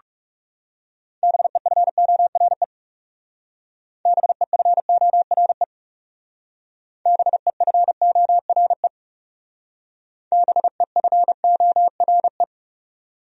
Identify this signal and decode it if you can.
Morse code, heard as a keyed tone, about 38 wpm, 700 Hz. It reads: BEFORE BEFORE BEFORE BEFORE